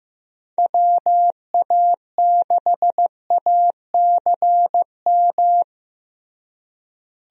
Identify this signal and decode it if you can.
Morse code, heard as a keyed tone, about 15 wpm, 700 Hz. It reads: WA6ACM
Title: USA Call Signs - 6 Characters - Repeated 3x - 15wpm